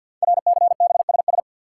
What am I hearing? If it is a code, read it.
Morse code, heard as a keyed tone, about 48 wpm, 700 Hz. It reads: RCBSS
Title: Code Groups - 5 Letters - 50wpm